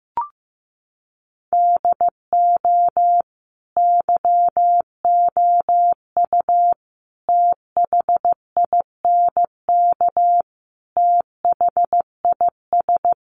Morse code, heard as a keyed tone, about 15 wpm, 700 Hz. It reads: DO YOU THINK THIS
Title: Sentences from Top 1000 Words 15wpm